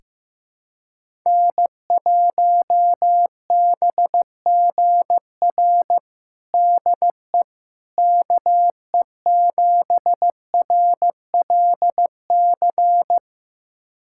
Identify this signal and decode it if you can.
Morse code, heard as a keyed tone, about 15 wpm, 700 Hz. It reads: N1BGR DE KE7RLC
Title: USA Callsign DE Callsign 15wpm